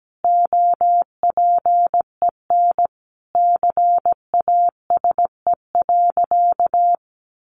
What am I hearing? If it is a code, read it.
Morse code, heard as a keyed tone, about 17 wpm, 700 Hz. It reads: OPEN CASE.
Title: Sentences from Top 1000 Words 17wpm